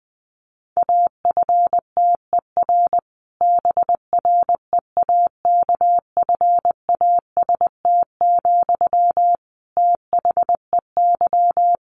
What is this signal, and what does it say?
Morse code, heard as a keyed tone, about 20 wpm, 700 Hz. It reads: AFTER BREAKFAST, THEY
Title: Sentences from Top 1500 Words 20wpm